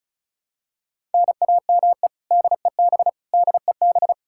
Morse code, heard as a keyed tone, about 36 wpm, 700 Hz. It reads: NAME DEB DEB
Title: Practice QSO - Exchange 1 - Beginning - 35wpm